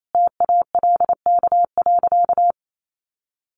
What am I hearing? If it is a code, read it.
Morse code, heard as a keyed tone, about 28 wpm, 700 Hz. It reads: TALK.
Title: Sentences from Top 1300 Words 28wpm